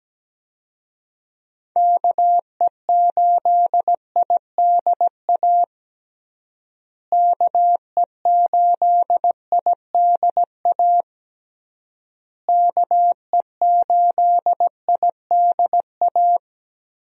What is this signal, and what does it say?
Morse code, heard as a keyed tone, about 17 wpm, 700 Hz. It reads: KE8IDA KE8IDA KE8IDA
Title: USA Call Signs - 6 Characters - Repeated 3x - 17wpm